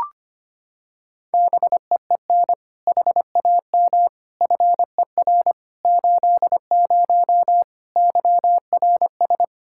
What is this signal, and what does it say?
Morse code, heard as a keyed tone, about 25 wpm, 700 Hz. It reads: BEEN HAM FER 80 YRS
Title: K1USN SST QSO Practice 25wpm